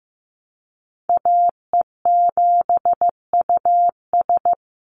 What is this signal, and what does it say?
Morse code, heard as a keyed tone, about 15 wpm, 700 Hz. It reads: AE7US
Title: Call Signs Repeated 3x - 15wpm